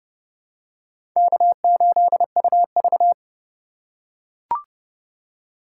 Morse code, heard as a keyed tone, about 30 wpm, 700 Hz. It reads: K8UV
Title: USA Call Signs - 4 Characters - 6x Character Spacing - 30wpm